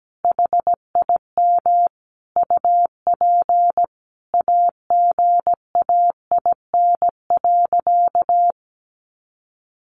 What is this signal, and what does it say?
Morse code, heard as a keyed tone, about 17 wpm, 700 Hz. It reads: HIM UP AGAIN.
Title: Sentences from Top 1000 Words 17wpm